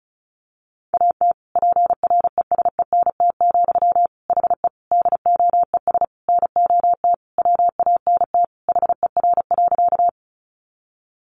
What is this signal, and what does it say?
Morse code, heard as a keyed tone, about 34 wpm, 700 Hz. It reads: AT PRESENT, HE DOES NOT WANT HER.
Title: Sentences from Top 800 Words 35wpm